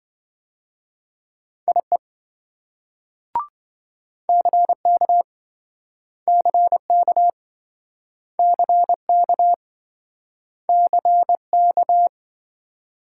Morse code, heard as a keyed tone, about 27 wpm, 700 Hz. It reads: IE CK CK CK CK